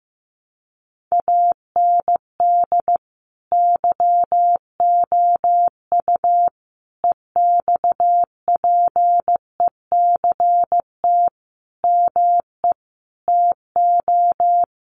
Morse code, heard as a keyed tone, about 15 wpm, 700 Hz. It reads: AND YOU EXPECT ME TO
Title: Sentences from Top 1100 Words 15wpm